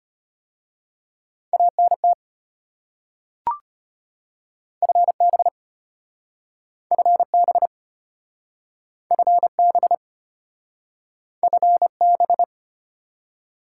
Morse code, heard as a keyed tone, about 32 wpm, 700 Hz. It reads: ANT FB FB FB FB